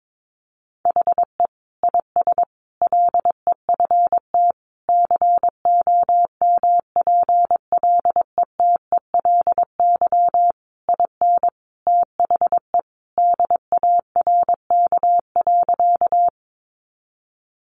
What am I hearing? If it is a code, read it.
Morse code, heard as a keyed tone, about 22 wpm, 700 Hz. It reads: HE IS LEFT COMPLETELY IN THE DARK.
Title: Sentences from Top 1100 Words 22wpm